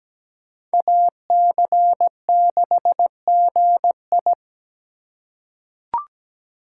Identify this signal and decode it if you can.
Morse code, heard as a keyed tone, about 17 wpm, 700 Hz. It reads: AC6GI
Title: USA Call Signs - 4x Character Spacing - 17wpm